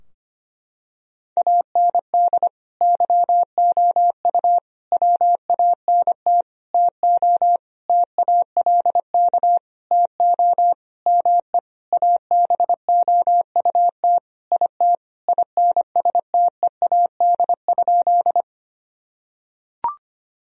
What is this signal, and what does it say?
Morse code, heard as a keyed tone, about 25 wpm, 700 Hz. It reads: AND YOU WANT TO TALK TO ME ABOUT IT INSTEAD?
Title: Sentences from Top 1000 Words - Review Part 4 - 25wpm